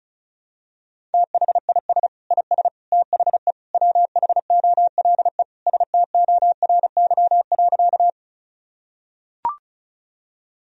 Morse code, heard as a keyed tone, about 34 wpm, 700 Hz. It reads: THIS IS THE WHOLE STORY.
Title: Sentences from Top 1000 Words - Review Part 1 - 35wpm